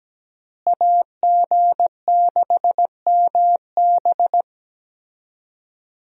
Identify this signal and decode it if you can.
Morse code, heard as a keyed tone, about 17 wpm, 700 Hz. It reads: AG6MB
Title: USA Call Signs - 5 Characters - Repeated 2x - 17wpm